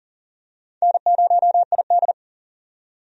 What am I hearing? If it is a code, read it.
Morse code, heard as a keyed tone, about 39 wpm, 700 Hz. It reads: N0ID
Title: USA Call Signs - 4 Characters - 40wpm